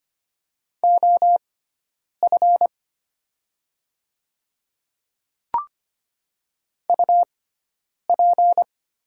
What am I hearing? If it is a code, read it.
Morse code, heard as a keyed tone, about 25 wpm, spaced as for 9 wpm, 700 Hz. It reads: OF UP
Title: 6x Character Spacing - Top 100 Words - 25wpm